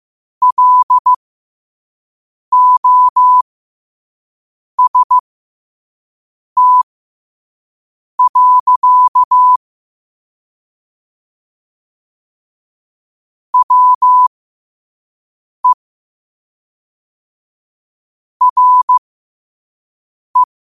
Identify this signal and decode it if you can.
Morse code, heard as a keyed tone, about 15 wpm, spaced as for 5 wpm, 1000 Hz. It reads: LOST. WE RE
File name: qotdfiles / 5wpm